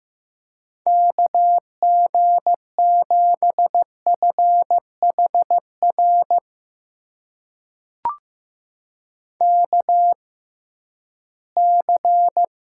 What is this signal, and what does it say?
Morse code, heard as a keyed tone, about 15 wpm, spaced as for 11 wpm, 700 Hz. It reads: KG7FHR KC
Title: USA Call Signs - 6 Characters - 6x Character Spacing - 15wpm